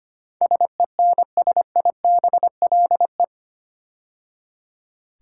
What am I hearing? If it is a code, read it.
Morse code, heard as a keyed tone, about 25 wpm, 700 Hz. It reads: SENSIBLE
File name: 8 Letter Words - First 3 Letters Given - 25wpm